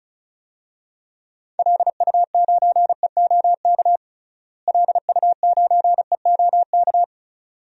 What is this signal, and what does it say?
Morse code, heard as a keyed tone, about 36 wpm, 700 Hz. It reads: LU9EOK LU9EOK
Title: DXCC-100 Call Signs - 6 Characters - Repeated 2x - 35wpm